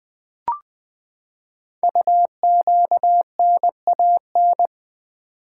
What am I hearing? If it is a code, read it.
Morse code, heard as a keyed tone, about 20 wpm, 700 Hz. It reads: UQNAN
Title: Code Groups - 5 Letters - 20wpm